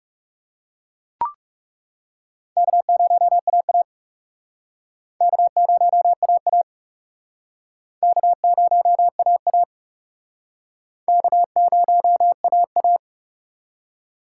Morse code, heard as a keyed tone, about 40 wpm, 700 Hz. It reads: K0AA K0AA K0AA K0AA